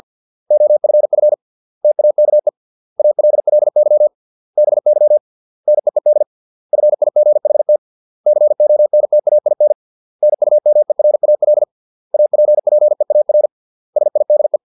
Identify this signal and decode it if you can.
Morse code, heard as a keyed tone, about 50 wpm, 600 Hz. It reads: OFF TAKE ALLY BY NEED FIGHT CONTAIN NUMERAL APPEAR SIDE
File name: 632 words / 632 Words 50WPM